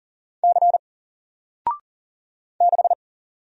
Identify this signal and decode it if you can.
Morse code, heard as a keyed tone, about 40 wpm, 700 Hz. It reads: C 6
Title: Single Letter-Number Alt 40wpm